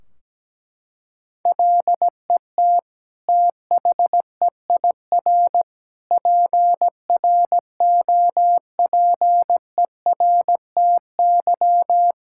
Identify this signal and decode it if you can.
Morse code, heard as a keyed tone, about 17 wpm, 700 Hz. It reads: LET THEIR PROPERTY
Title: Sentences from Top 1600 Words 17wpm